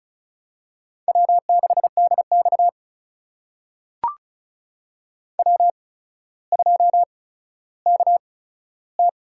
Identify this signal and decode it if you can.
Morse code, heard as a keyed tone, about 36 wpm, spaced as for 15 wpm, 700 Hz. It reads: W6DX W2KT
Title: USA Call Signs - 4 Characters - 8x Character Spacing - 35wpm